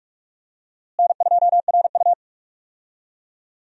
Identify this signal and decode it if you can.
Morse code, heard as a keyed tone, about 46 wpm, 700 Hz. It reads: N2RU
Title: USA Call Signs - 4 Characters - Repeated 3x - 45wpm